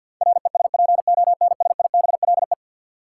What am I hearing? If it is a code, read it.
Morse code, heard as a keyed tone, about 53 wpm, 700 Hz. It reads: RESPONSIBLE
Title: Sentences from Top 1000 Words - Review Part 7 - New Word Drill - 50wpm